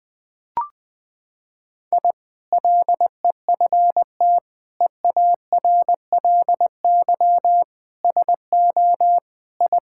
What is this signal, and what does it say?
Morse code, heard as a keyed tone, about 20 wpm, 700 Hz. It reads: I LEFT EARLY SO I
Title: Sentences with 2-Word Phrases - 20wpm